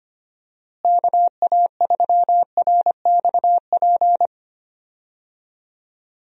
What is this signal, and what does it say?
Morse code, heard as a keyed tone, about 25 wpm, 700 Hz. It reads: KA3RXP
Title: USA Call Signs - 6 Characters - Repeated 3x - 25wpm